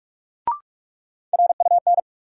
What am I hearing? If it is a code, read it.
Morse code, heard as a keyed tone, about 46 wpm, 700 Hz. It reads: RUN
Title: Sentences from Top 300 Words - New Word Drill - 45wpm